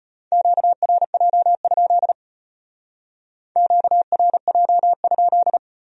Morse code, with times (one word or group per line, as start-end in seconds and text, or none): 0.3-2.1 QRJ?
3.6-5.6 QRJ?